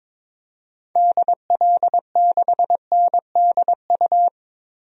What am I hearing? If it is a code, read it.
Morse code, heard as a keyed tone, about 22 wpm, 700 Hz. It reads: DL6NDU
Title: DXCC-100 Call Signs - 6 Characters - Repeated 2x - 22wpm